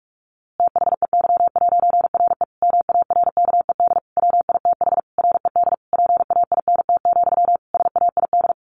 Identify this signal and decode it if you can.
Morse code, heard as a keyed tone, about 44 wpm, 700 Hz. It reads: THEY'RE MARKED WITH RED PAINT, SAID
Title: A Sound Of Thunder 45wpm